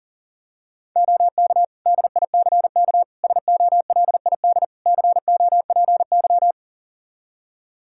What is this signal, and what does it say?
Morse code, heard as a keyed tone, about 37 wpm, 700 Hz. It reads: OK DICK SOLID COPY
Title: Practice QSO - Exchange 3 - More Info - 40wpm